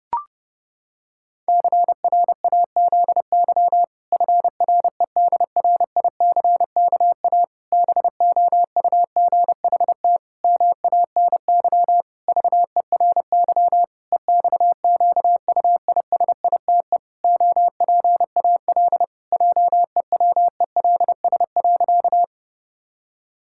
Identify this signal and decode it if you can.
Morse code, heard as a keyed tone, about 30 wpm, 700 Hz. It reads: CRAZY FREDRICKA BOUGHT MANY VERY EXQUISITE OPAL JEWELS.
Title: Pangram Challenge v2 - 30wpm